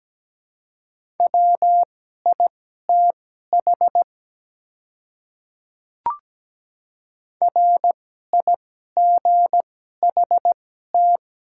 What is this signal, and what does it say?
Morse code, heard as a keyed tone, about 17 wpm, spaced as for 12 wpm, 700 Hz. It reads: WITH RIGHT
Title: 2x Character Spacing - Top 300 Words - 17wpm